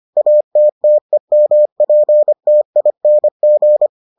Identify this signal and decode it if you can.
Morse code, heard as a keyed tone, about 25 wpm, 600 Hz. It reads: ATTEMPTING